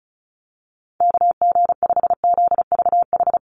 Morse code, heard as a keyed tone, about 34 wpm, 700 Hz. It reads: KG5ZVH
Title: USA Call Signs Repeated 2x 35wpm